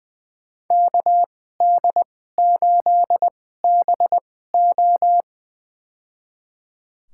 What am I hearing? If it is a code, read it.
Morse code, heard as a keyed tone, about 20 wpm, spaced as for 15 wpm, 700 Hz. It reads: KD8BO
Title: USA Call Signs - 2x Character Spacing - 20wpm